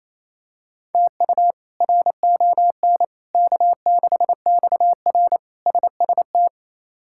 Morse code, heard as a keyed tone, about 28 wpm, 700 Hz. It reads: TU RON K6XR SST
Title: K1USN SST QSO Practice - 28wpm